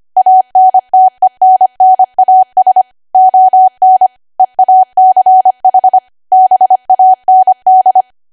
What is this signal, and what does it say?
Morse code, heard as a keyed tone, about 25 wpm, 750 Hz. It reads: ANTENNAS ON EACH BAND
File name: cwa / W1AW-Level3 / 25WPM